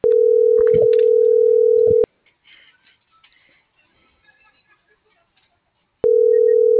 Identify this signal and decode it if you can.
Ringback tone (North America), 440 + 480 Hz.
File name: CENTRAL GENERAL / out rubyalejandra